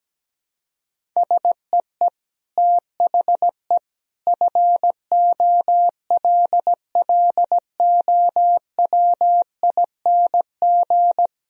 Morse code, heard as a keyed tone, about 17 wpm, 700 Hz. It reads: SEE THE FOLLOWING
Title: Sentences from Top 1000 Words - Review Part 1 - 17wpm